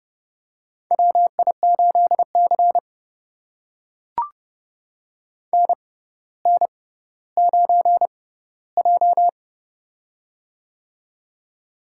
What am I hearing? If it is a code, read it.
Morse code, heard as a keyed tone, about 30 wpm, spaced as for 15 wpm, 700 Hz. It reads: WI8C NN9J
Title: USA Call Signs - 4 Characters - 6x Character Spacing - 30wpm